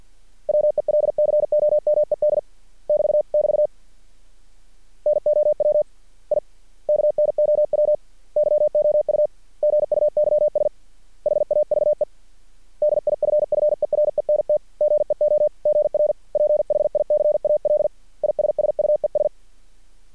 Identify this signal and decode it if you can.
Morse code, heard as a keyed tone, about 50 wpm, 600 Hz. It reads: WELCOMED <BT> <BT> NOW I KNOW YOU GUYS HAVE DIFFERENT GEOGRPHICAL ISSUES
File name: archived / CRSnet